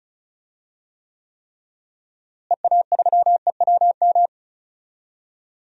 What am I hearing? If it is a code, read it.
Morse code, heard as a keyed tone, about 36 wpm, 700 Hz. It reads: EA3EWM
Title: DXCC-100 Call Signs - 6 Characters - Repeated 3x - 35wpm